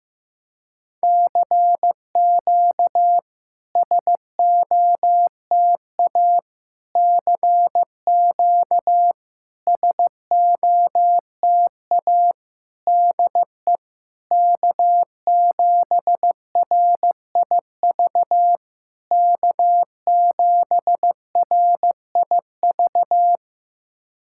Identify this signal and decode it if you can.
Morse code, heard as a keyed tone, about 15 wpm, 690 Hz. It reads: CQ SOTA CQ SOTA DE K7RIV K7RIV
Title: SOTA QSO Practice 15wpm